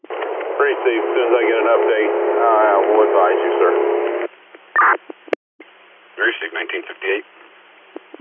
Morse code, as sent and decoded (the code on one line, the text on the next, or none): none